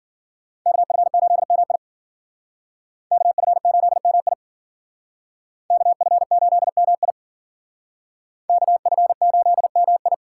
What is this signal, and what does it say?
Morse code, heard as a keyed tone, about 51 wpm, 700 Hz. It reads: KF8MI KF8MI KF8MI KF8MI